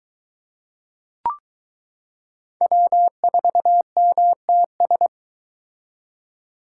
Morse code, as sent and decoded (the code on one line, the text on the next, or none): .-- ....- -- - ...
W4MTS